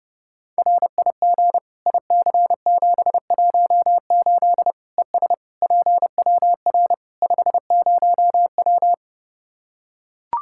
Morse code, heard as a keyed tone, about 30 wpm, 700 Hz. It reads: RIG IC718 ES PWR 50W
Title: Practice QSO - Exchange 2 - Info - 30wpm